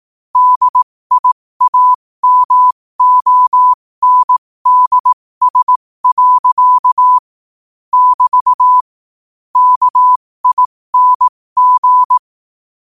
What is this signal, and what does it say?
Morse code, heard as a keyed tone, about 18 wpm, spaced as for 15 wpm, 1000 Hz. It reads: DIAMONDS. <BT> KING
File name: qotdfiles / 15wpm